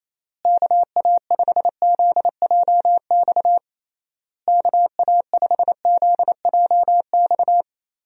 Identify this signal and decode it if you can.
Morse code, heard as a keyed tone, about 28 wpm, 700 Hz. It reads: KA5ZJX KA5ZJX
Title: USA Call Signs - 6 Characters - Repeated 3x - 28wpm